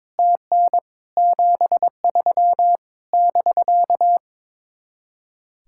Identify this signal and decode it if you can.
Morse code, heard as a keyed tone, about 22 wpm, 700 Hz. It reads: TN 73 <BK>
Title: Speed-Racing Common QSO Phrases Encore - 22wpm